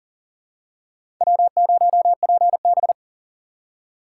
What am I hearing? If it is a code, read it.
Morse code, heard as a keyed tone, about 40 wpm, 700 Hz. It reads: W0PB
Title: USA Call Signs - 4 Characters - 2x Character Spacing - 40wpm